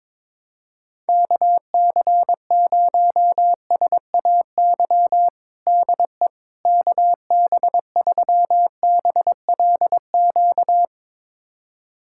Morse code, heard as a keyed tone, about 21 wpm, 700 Hz. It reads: KC0SAY DE KB3BLQ
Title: USA Callsign DE Callsign - 6 Characters - 22wpm